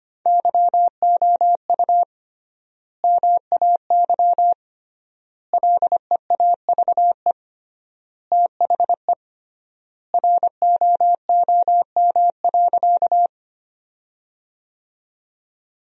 Morse code, heard as a keyed tone, about 25 wpm, 700 Hz. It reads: YOU MAY LEAVE THE ROOM.
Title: 3x Word-Spacing - Sentences from Top 400 Words - 25wpm